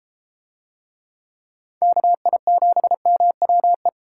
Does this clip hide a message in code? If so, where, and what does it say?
Yes: Morse code from 1.8 to 3.9 s, heard as a keyed tone, about 32 wpm, 700 Hz. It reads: KI7MWE